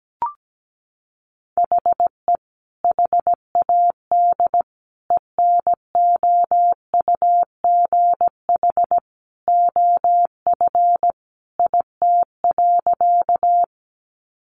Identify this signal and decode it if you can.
Morse code, heard as a keyed tone, about 17 wpm, 700 Hz. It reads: HE HAD ENOUGH OF IT.
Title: Sentences from Top 400 Words 17wpm